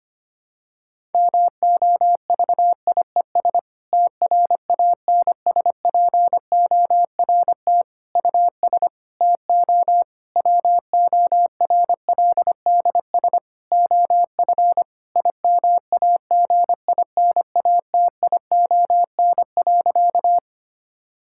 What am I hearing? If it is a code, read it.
Morse code, heard as a keyed tone, about 25 wpm, 700 Hz. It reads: MOVIES TRANSPORT US TO WORLDS OF IMAGINATION.
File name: practice / 25wpm / Difficult Sentences with Hints - 25wpm